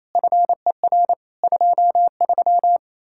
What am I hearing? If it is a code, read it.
Morse code, heard as a keyed tone, about 34 wpm, 700 Hz. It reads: FER 23
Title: Speed-Racing Common QSO Phrases Encore - 28wpm